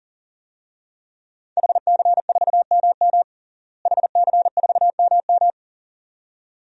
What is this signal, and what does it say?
Morse code, heard as a keyed tone, about 37 wpm, 700 Hz. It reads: HC4MM HC4MM
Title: DXCC-100 Call Signs - 5 Characters - Repeated 2x - 40wpm